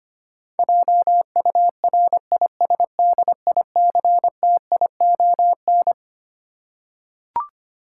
Morse code, heard as a keyed tone, about 25 wpm, 700 Hz. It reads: JURISDICTION
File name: Twelve Letter Words - 25wpm